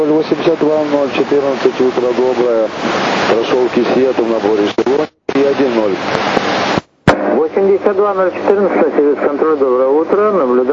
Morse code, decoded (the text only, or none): none